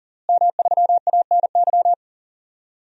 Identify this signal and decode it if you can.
Morse code, heard as a keyed tone, about 40 wpm, 700 Hz. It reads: M3ANY
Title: DXCC-100 Call Signs - 5 Characters - Repeated 3x - 40wpm